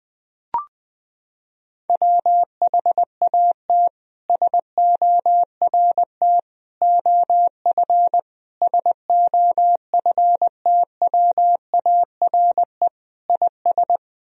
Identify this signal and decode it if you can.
Morse code, heard as a keyed tone, about 20 wpm, 700 Hz. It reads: WHAT SORT OF SOFTWARE IS
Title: Sentences from Top 1000 Words - Review Part 3 - 20wpm